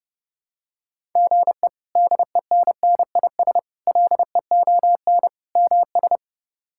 Morse code, heard as a keyed tone, about 31 wpm, 700 Hz. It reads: GE DENNIS LEON MS
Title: K1USN SST QSO Practice - 30wpm